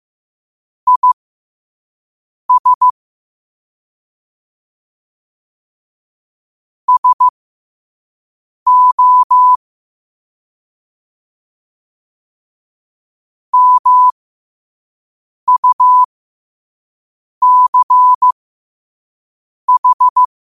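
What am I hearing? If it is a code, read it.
Morse code, heard as a keyed tone, about 15 wpm, spaced as for 5 wpm, 1000 Hz. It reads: IS SO MUCH
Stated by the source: Morse code quote of the day at 5 wpm.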